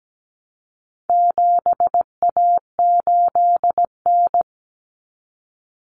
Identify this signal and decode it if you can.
Morse code, heard as a keyed tone, about 17 wpm, 700 Hz. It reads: 7A8N
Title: Common Cut Numbers - 4-Digit 17wpm